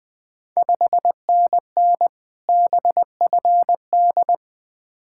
Morse code, heard as a keyed tone, about 20 wpm, 690 Hz. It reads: 5NN BFD
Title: Ontario QSO Party Exchange 20wpm